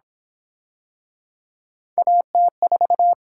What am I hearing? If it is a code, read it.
Morse code, heard as a keyed tone, about 29 wpm, 700 Hz. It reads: AT4